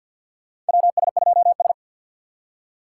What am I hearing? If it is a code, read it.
Morse code, heard as a keyed tone, about 47 wpm, 700 Hz. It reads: WS2S
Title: USA Call Signs - 4 Characters - 2x Character Spacing - 50wpm